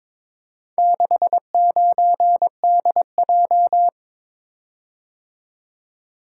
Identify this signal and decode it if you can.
Morse code, heard as a keyed tone, about 22 wpm, 700 Hz. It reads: 69DJ
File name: practice / 22wpm / Four Letter-Number Combinations - 22wpm